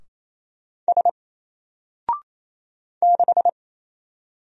Morse code, heard as a keyed tone, about 27 wpm, 700 Hz. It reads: S 6
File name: Single-Letter-Number-28wpm